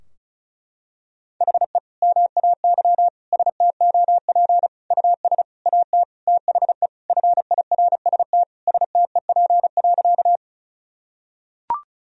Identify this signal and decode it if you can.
Morse code, heard as a keyed tone, about 36 wpm, 700 Hz. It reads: HE MAY STOP US AT THE FIRST STEP.
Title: Sentences from Top 1000 Words - Review Part 2 - 35wpm